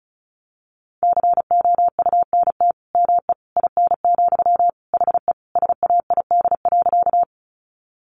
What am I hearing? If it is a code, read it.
Morse code, heard as a keyed tone, about 34 wpm, 700 Hz. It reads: COUNT ME IN, HE SAID.
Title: Sentences from Top 1500 Words 35wpm